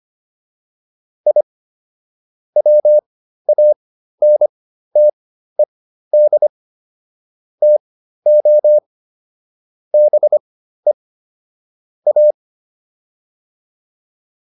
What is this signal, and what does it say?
Morse code, heard as a keyed tone, about 25 wpm, spaced as for 13 wpm, 600 Hz. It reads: I WANTED TO BE A
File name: WD404 13